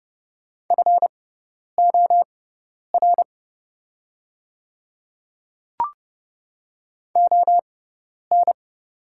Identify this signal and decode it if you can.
Morse code, heard as a keyed tone, about 30 wpm, spaced as for 10 wpm, 700 Hz. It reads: FOR ON
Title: 6x Character Spacing - Top 100 Words - 30wpm